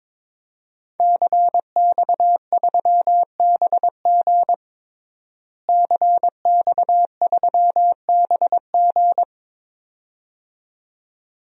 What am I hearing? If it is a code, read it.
Morse code, heard as a keyed tone, about 22 wpm, 700 Hz. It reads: CX3BG CX3BG
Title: DXCC-100 Call Signs - 5 Characters - Repeated 2x - 22wpm